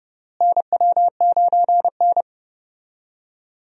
Morse code, heard as a keyed tone, about 32 wpm, 700 Hz. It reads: NW9N
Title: USA Call Signs - 4 Characters - Repeated 3x - 30wpm